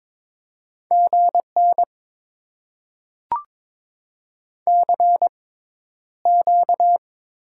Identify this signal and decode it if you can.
Morse code, heard as a keyed tone, about 21 wpm, spaced as for 11 wpm, 700 Hz. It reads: GN CQ
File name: Top 90 QSO Elements - 6x Character Spacing - 22wpm